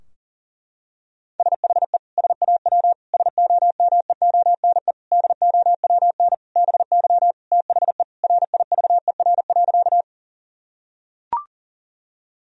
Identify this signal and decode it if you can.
Morse code, heard as a keyed tone, about 40 wpm, 700 Hz. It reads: SHE SAW SOMEONE DOWN BY THE RIVER.
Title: Sentences from Top 1000 Words - Review Part 4 - 40wpm